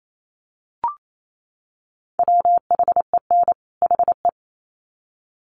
Morse code, heard as a keyed tone, about 28 wpm, 700 Hz. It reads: WHEN HE
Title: Sets of 2 Words 28wpm